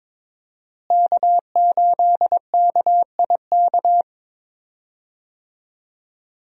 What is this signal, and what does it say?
Morse code, heard as a keyed tone, about 21 wpm, 700 Hz. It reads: K8KIK
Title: USA Call Signs - 5 Characters - Repeated 2x - 22wpm